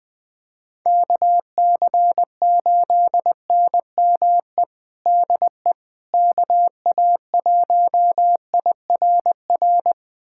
Morse code, heard as a keyed tone, about 21 wpm, 700 Hz. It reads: KC8NME DE KA1IRR
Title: USA Callsign DE Callsign - 6 Characters - 20wpm